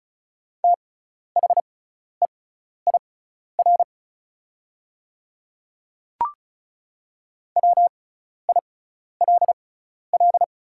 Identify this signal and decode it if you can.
Morse code, heard as a keyed tone, about 36 wpm, spaced as for 12 wpm, 700 Hz. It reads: THEIR WILL
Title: 6x Character Spacing - Top 100 Words - 35wpm